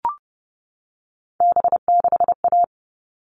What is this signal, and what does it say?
Morse code, heard as a keyed tone, about 30 wpm, 700 Hz. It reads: B6A
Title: All Cut Numbers - 3-Digit 30wpm